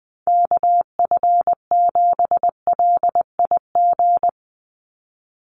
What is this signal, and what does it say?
Morse code, heard as a keyed tone, about 20 wpm, 700 Hz. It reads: KF7LIG
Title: USA Call Signs 20wpm